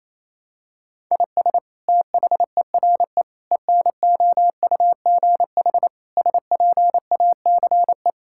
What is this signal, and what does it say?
Morse code, heard as a keyed tone, about 27 wpm, 700 Hz. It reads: IS THERE ENOUGH SPACE
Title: Sentences from Top 1000 Words - Review Part 5 - 28wpm